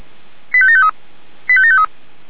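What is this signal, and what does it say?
Five-tone selective call (ZVEI), heard as quick runs of five tones, tones of about 70 ms. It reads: 85753 85753
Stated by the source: SDR-received ZVEI five-tone selective call, code 89123.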